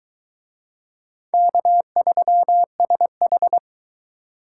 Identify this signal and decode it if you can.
Morse code, heard as a keyed tone, about 20 wpm, 700 Hz. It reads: K3SH